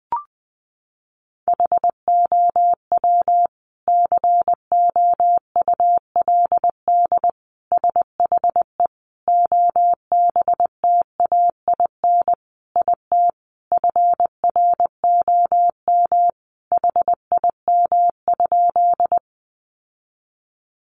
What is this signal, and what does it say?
Morse code, heard as a keyed tone, about 20 wpm, 700 Hz. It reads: HOW COULD SHE OBTAIN IT FROM HIM?
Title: Sentences from Top 1300 Words 20wpm